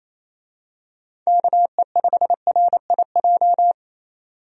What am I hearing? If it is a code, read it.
Morse code, heard as a keyed tone, about 27 wpm, 700 Hz. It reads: KE5RIJ
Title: USA Call Signs - 6 Characters - 28wpm